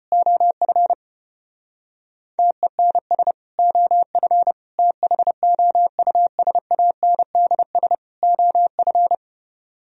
Morse code, heard as a keyed tone, about 32 wpm, 700 Hz. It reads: OF TENS OF THOUSANDS OF